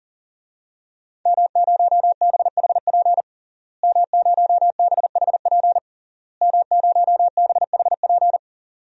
Morse code, heard as a keyed tone, about 40 wpm, 700 Hz. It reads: M0BHP M0BHP M0BHP
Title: DXCC-100 Call Signs - 5 Characters - Repeated 3x - 40wpm